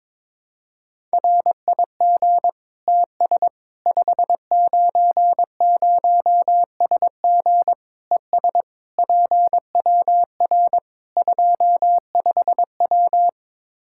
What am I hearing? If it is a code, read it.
Morse code, heard as a keyed tone, about 22 wpm, 700 Hz. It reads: RIG TS 590SG ES PWR 25W
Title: K1USN SST QSO Practice 22wpm